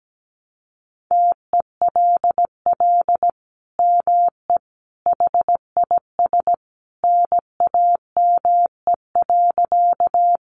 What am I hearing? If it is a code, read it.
Morse code, heard as a keyed tone, about 17 wpm, 700 Hz. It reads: TELL ME HIS NAME.
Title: Sentences from Top 200 Words 17wpm